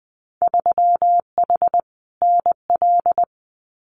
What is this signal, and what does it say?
Morse code, heard as a keyed tone, about 20 wpm, 700 Hz. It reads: 3H NL
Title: Winter Field Day Exchanges Encore 20wpm